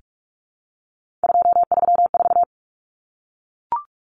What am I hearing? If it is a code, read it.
Morse code, heard as a keyed tone, about 44 wpm, 700 Hz. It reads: 234
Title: Numbers - 3-Digit 45wpm